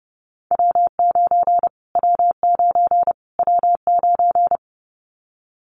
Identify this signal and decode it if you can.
Morse code, heard as a keyed tone, about 30 wpm, 700 Hz. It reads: W9 W9 W9
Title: Callsign Prefixes for Top 25 DXCC - Repeated 3x - 30wpm